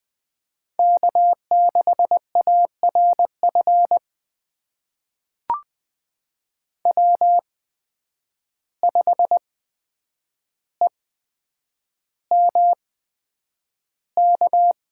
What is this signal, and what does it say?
Morse code, heard as a keyed tone, about 20 wpm, spaced as for 9 wpm, 700 Hz. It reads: K6ARF W5EMK
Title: USA Call Signs - 5 Characters - 8x Character Spacing - 20wpm